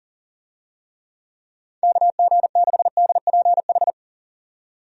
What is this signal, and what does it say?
Morse code, heard as a keyed tone, about 40 wpm, 700 Hz. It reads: KG6DPH
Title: USA Call Signs - 6 Characters - 40wpm